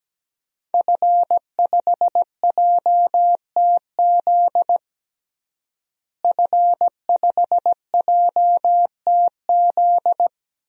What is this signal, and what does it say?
Morse code, heard as a keyed tone, about 17 wpm, 700 Hz. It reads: F5JTZ F5JTZ
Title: DXCC-100 Call Signs - 5 Characters - Repeated 2x - 17wpm